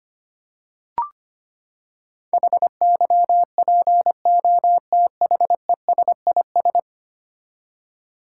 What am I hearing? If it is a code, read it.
Morse code, heard as a keyed tone, about 25 wpm, 700 Hz. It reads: HYPOTHESIS
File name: Ten Letter Words - 25wpm